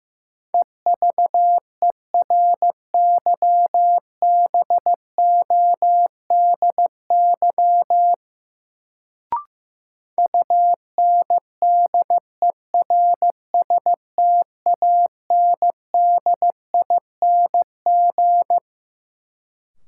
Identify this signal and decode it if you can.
Morse code, heard as a keyed tone, about 15 wpm, 700 Hz. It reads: EVERYBODY UNDERSTANDING
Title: Sentences from Top 1400 Words - New Word Drill Alt - 15wpm